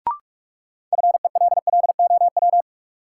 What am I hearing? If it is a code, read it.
Morse code, heard as a keyed tone, about 46 wpm, 700 Hz. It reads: FELLOW
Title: Sets of 5 Words Encore - New Word Drill - 45wpm